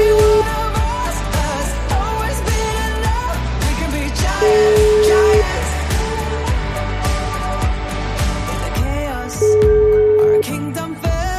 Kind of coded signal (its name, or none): telephone call progress tone